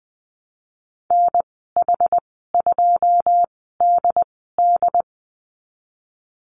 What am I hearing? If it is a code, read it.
Morse code, heard as a keyed tone, about 20 wpm, spaced as for 15 wpm, 700 Hz. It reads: NH2DD
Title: 122 Lesson 20g - Number 3 - Call Signs 20wpm